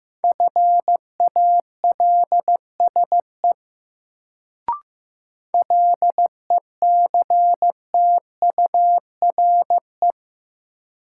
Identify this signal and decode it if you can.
Morse code, heard as a keyed tone, about 15 wpm, 700 Hz. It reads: FALSE LECTURE
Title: Sentences from Top 1600 Words - New Word Drill - 15wpm